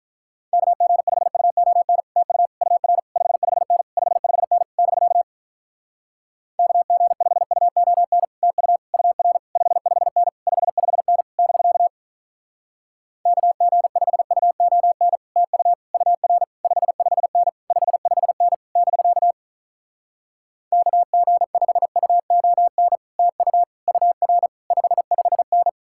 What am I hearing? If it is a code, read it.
Morse code, heard as a keyed tone, about 47 wpm, 700 Hz. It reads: KG5UON TU UR 55N 55N <BK> KG5UON TU UR 55N 55N <BK> KG5UON TU UR 55N 55N <BK> KG5UON TU UR 55N